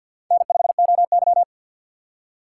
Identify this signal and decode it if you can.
Morse code, heard as a keyed tone, about 51 wpm, 700 Hz. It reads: N5OY
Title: USA Call Signs - 4 Characters - 2x Character Spacing - 50wpm